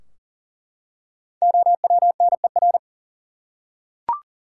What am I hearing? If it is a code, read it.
Morse code, heard as a keyed tone, about 40 wpm, 700 Hz. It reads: OWNER
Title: Five Letter Words - 40wpm